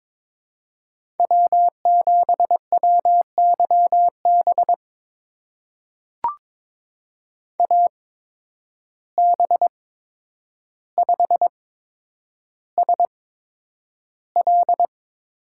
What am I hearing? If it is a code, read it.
Morse code, heard as a keyed tone, about 23 wpm, spaced as for 10 wpm, 700 Hz. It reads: W7WYB AB5SL
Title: USA Call Signs - 5 Characters - 8x Character Spacing - 22wpm